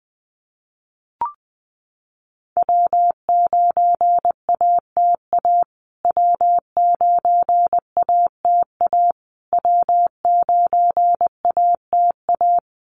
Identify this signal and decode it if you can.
Morse code, heard as a keyed tone, about 20 wpm, 700 Hz. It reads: W9ATA W9ATA W9ATA
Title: USA Call Signs Repeated 3x 20wpm